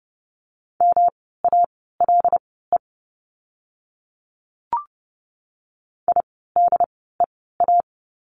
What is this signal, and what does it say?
Morse code, heard as a keyed tone, about 30 wpm, spaced as for 17 wpm, 700 Hz. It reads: MALE IDEA
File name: course / 30:10wpm / 108 Lesson 18f - Letter m - Words 30wpm